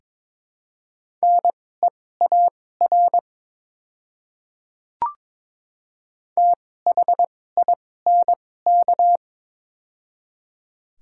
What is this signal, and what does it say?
Morse code, heard as a keyed tone, about 22 wpm, spaced as for 16 wpm, 700 Hz. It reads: NEAR THINK
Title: 2x Character Spacing - Top 500 Words - 22wpm